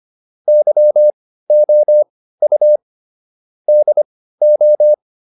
Morse code, heard as a keyed tone, about 25 wpm, spaced as for 15 wpm, 600 Hz. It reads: YOU DO